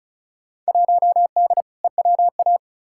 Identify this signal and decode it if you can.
Morse code, heard as a keyed tone, about 36 wpm, 700 Hz. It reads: 1D EWA
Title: Field Day Exchanges - 35wpm